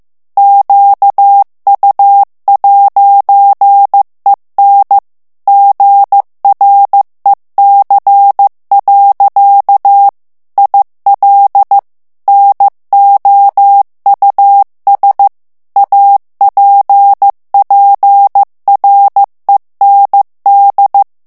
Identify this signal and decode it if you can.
Morse code, heard as a keyed tone, about 15 wpm, 800 Hz. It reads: QU'EN GREC. IL NOUS APPREND
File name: morse mp3 / FRANCAIS TEXTE CLAIR 2 VITESSE 14 MOTS MINUTES